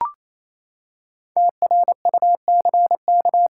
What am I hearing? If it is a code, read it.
Morse code, heard as a keyed tone, about 28 wpm, 700 Hz. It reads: TRUCK
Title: Five Letter Words - 28wpm